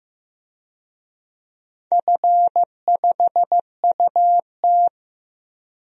Morse code, heard as a keyed tone, about 15 wpm, 700 Hz. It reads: F5UT
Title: DXCC-100 Call Signs - 4 Characters - 15wpm